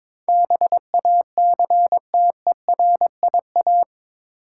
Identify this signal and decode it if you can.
Morse code, heard as a keyed tone, about 22 wpm, 700 Hz. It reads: BACTERIA
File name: practice / 22wpm / Eight Letter Words - 22wpm